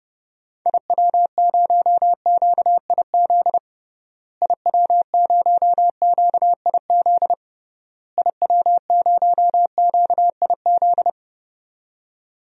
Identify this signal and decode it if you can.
Morse code, heard as a keyed tone, about 32 wpm, 700 Hz. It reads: IW0QIZ IW0QIZ IW0QIZ
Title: DXCC-100 Call Signs - 6 Characters - Repeated 3x - 30wpm